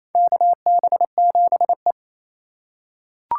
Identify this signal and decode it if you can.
Morse code, heard as a keyed tone, about 28 wpm, 700 Hz. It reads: KB7E
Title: USA Call Signs - 4 Characters - 4x Character Spacing - 28wpm